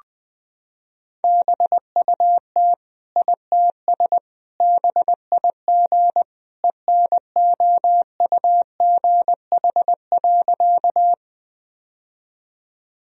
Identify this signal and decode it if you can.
Morse code, heard as a keyed tone, about 21 wpm, 700 Hz. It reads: BUT ITS BIG ENOUGH.
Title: Sentences from Top 1000 Words - Review Part 1 - 20wpm